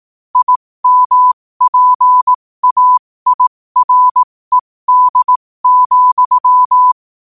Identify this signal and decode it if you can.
Morse code, heard as a keyed tone, about 18 wpm, spaced as for 15 wpm, 1000 Hz. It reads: IMPAIRED,
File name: qotdfiles / 15wpm